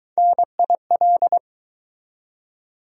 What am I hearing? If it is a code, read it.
Morse code, heard as a keyed tone, about 23 wpm, 700 Hz. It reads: NIL